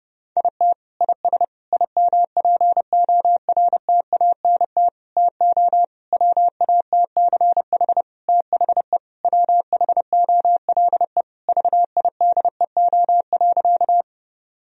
Morse code, heard as a keyed tone, about 28 wpm, 700 Hz. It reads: IT IS IMPORTANT TO WATCH THE WHOLE VIDEO.
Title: Sentences from Top 1000 Words - Review Part 4 - 30wpm